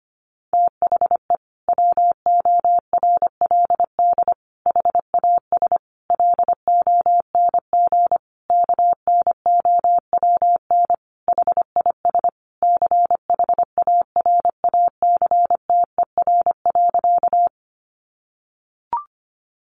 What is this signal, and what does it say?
Morse code, heard as a keyed tone, about 25 wpm, 700 Hz. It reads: THE WORLD HAS LONG KNOWN HIS CHARACTER.
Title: Sentences from Top 1200 Words 25wpm